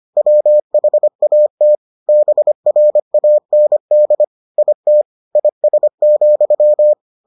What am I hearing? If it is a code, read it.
Morse code, heard as a keyed tone, about 25 wpm, 600 Hz. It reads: WHAT BRAND IT IS,